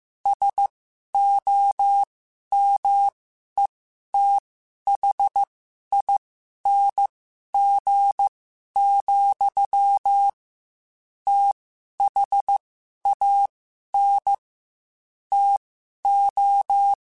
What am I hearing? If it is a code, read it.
Morse code, heard as a keyed tone, about 15 wpm, spaced as for 11 wpm, 800 Hz. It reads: SOMETHING, THAN TO
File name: Quotes / Quotes-mp3s / StandFor @13